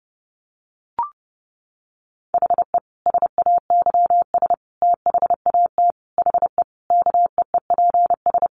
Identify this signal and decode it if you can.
Morse code, heard as a keyed tone, about 30 wpm, 700 Hz. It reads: HE SAYS THAT HE KEEPS
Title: Additional Easy Phrases 30wpm